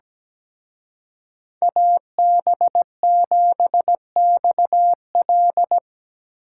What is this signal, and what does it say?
Morse code, heard as a keyed tone, about 18 wpm, 700 Hz. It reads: AB7XL
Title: Speed Racing - USA 2x2 Call Signs 17wpm